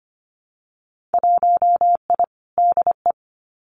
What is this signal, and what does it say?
Morse code, heard as a keyed tone, about 25 wpm, 700 Hz. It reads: 1I DE
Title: Winter Field Day Exchanges Encore 25wpm